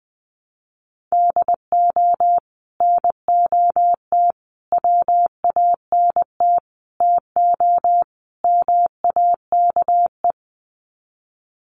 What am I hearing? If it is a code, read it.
Morse code, heard as a keyed tone, about 20 wpm, 700 Hz. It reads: DO NOT WANT TO MAKE
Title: Sets of 5 Words Encore 20wpm